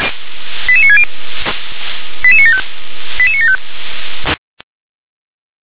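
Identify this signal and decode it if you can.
Five-tone selective call (ZVEI), heard as quick runs of five tones, tones of about 70 ms. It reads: 88960 88975 88975
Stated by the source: ZVEI five-tone selective call 88966 received on SDR.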